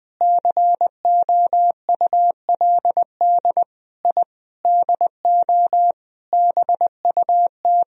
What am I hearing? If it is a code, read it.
Morse code, heard as a keyed tone, about 20 wpm, 700 Hz. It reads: COULD I DO BUT
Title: Sentences from Top 1000 Words - Review Part 1 - 20wpm